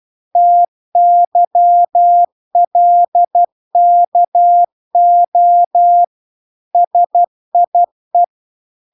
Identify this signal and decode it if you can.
Morse code, heard as a keyed tone, about 12 wpm, 700 Hz. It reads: TYLKO SIE